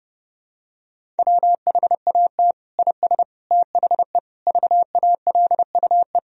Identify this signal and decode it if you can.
Morse code, heard as a keyed tone, about 30 wpm, 700 Hz. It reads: WHAT IS THE VALUE
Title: Sentences from Top 1000 Words - Review Part 3 - 30wpm